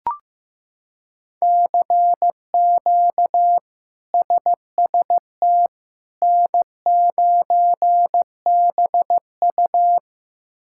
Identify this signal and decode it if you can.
Morse code, heard as a keyed tone, about 15 wpm, 700 Hz. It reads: CQ SST N9BU
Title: K1USN SST QSO Practice - 15wpm